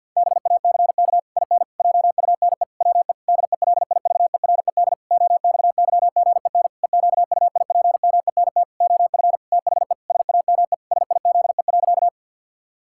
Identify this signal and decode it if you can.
Morse code, heard as a keyed tone, about 50 wpm, 700 Hz. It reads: BACK IN JUNE WE DELIVERED OXYGEN EQUIPMENT OF THE SAME SIZE.
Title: Pangram Challenge v2 - 50wpm